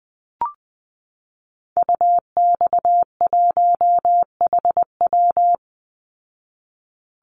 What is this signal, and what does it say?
Morse code, heard as a keyed tone, about 19 wpm, 700 Hz. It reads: UX1HW
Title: Call Signs Repeated 2x - 20wpm